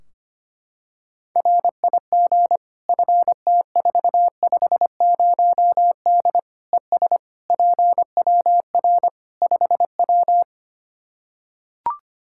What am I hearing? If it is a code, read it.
Morse code, heard as a keyed tone, about 25 wpm, 700 Hz. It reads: RIG FT450D ES PWR 5W
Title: Practice QSO - Exchange 2 - Info - 25wpm